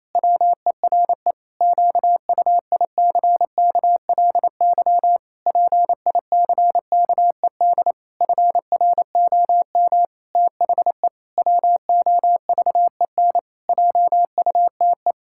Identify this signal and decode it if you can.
Morse code, heard as a keyed tone, about 28 wpm, 700 Hz. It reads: WERE QUICKLY PICKED FROM THE WOVEN JUTE
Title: Pangram Challenge v2 - 28wpm